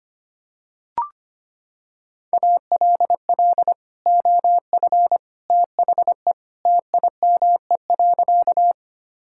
Morse code, heard as a keyed tone, about 25 wpm, 700 Hz. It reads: ALL OF THE TIME.
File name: practice / 25wpm / Sentences with 3-Word Phrases - 25wpm